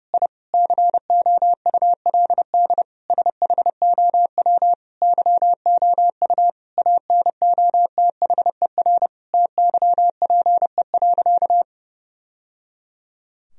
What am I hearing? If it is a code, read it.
Morse code, heard as a keyed tone, about 30 wpm, 700 Hz. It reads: I COULD SHOW YOU ANOTHER TYPE.
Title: Sentences from Top 1000 Words - Review Part 3 - 30wpm